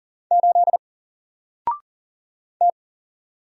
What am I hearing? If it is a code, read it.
Morse code, heard as a keyed tone, about 40 wpm, 700 Hz. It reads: 8 T
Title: Single Character (Letter:Number:Punctuation:Prosign) Alt 40wpm